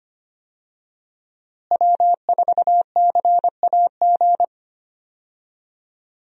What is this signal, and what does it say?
Morse code, heard as a keyed tone, about 25 wpm, 700 Hz. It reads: W4CAG
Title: USA Call Signs - 5 Characters - Repeated 2x - 25wpm